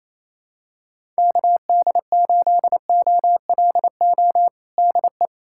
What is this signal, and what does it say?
Morse code, heard as a keyed tone, about 28 wpm, 700 Hz. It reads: KD8OLO DE
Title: USA Callsign DE Callsign 28wpm